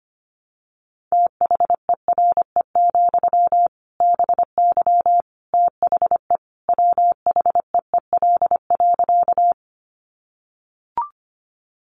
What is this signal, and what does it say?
Morse code, heard as a keyed tone, about 25 wpm, 700 Hz. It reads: THERE, BY THE WHEEL.
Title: Sentences from Top 500 Words 25wpm